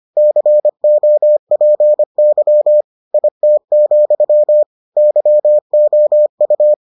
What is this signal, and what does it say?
Morse code, heard as a keyed tone, about 25 wpm, 600 Hz. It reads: COPY IT, YOU